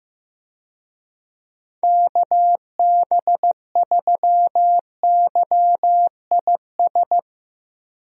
Morse code, heard as a keyed tone, about 15 wpm, 700 Hz. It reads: KB3YIS
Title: USA Call Signs - 6 Characters - 15wpm